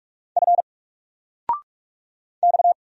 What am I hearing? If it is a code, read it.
Morse code, heard as a keyed tone, about 44 wpm, 700 Hz. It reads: F X
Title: Single Letter-Number 45wpm